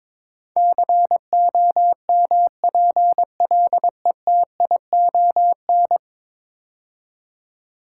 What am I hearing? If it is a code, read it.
Morse code, heard as a keyed tone, about 21 wpm, 700 Hz. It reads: COMPLETION
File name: practice / 22wpm / 10 Letter Words - First 3 Letters Given - 22wpm